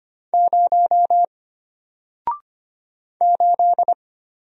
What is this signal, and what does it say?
Morse code, heard as a keyed tone, about 25 wpm, 700 Hz. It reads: 0 8
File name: Single-Letter-Number-25wpm